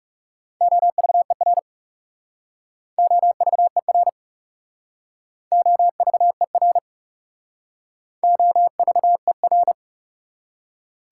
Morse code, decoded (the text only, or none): OVER OVER OVER OVER